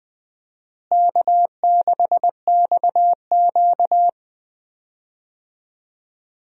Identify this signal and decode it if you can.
Morse code, heard as a keyed tone, about 20 wpm, 700 Hz. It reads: K6XQ
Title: USA Call Signs - 4 Characters - Repeated 3x - 20wpm